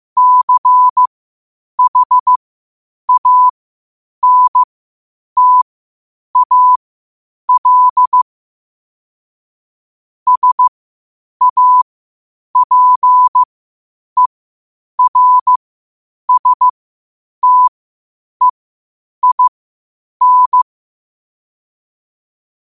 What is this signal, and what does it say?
Morse code, heard as a keyed tone, about 15 wpm, spaced as for 8 wpm, 1000 Hz. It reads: CHANTAL SAPERSTEIN
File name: 8wpm